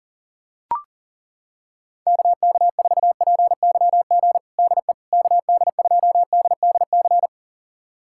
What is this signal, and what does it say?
Morse code, heard as a keyed tone, about 40 wpm, 700 Hz. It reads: KK4PYG DE KD2DDC
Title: USA Callsign DE Callsign 40wpm